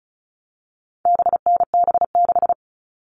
Morse code, heard as a keyed tone, about 34 wpm, 700 Hz. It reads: BNB6
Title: All Cut Numbers - 4-Digit 35wpm